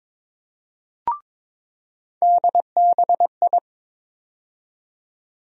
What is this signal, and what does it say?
Morse code, heard as a keyed tone, about 22 wpm, 700 Hz. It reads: DBI
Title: USA Call Sign Postfixes - 3 Letters 22wpm